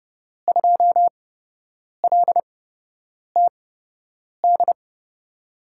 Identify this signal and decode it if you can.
Morse code, heard as a keyed tone, about 31 wpm, spaced as for 8 wpm, 700 Hz. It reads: 2LTD
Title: USA Call Signs - 5 Characters - 8x Character Spacing - 30wpm